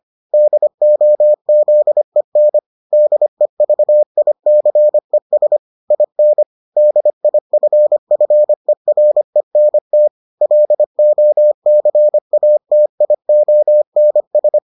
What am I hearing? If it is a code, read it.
Morse code, heard as a keyed tone, about 25 wpm, 600 Hz. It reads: DOZEN DEVICES IN DIFFERENT LOCATIONS